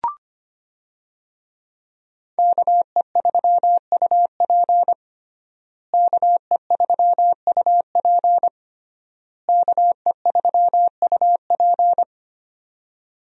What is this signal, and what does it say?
Morse code, heard as a keyed tone, about 25 wpm, 700 Hz. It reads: KE3UP KE3UP KE3UP
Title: USA Call Signs - 5 Characters - Repeated 3x - 25wpm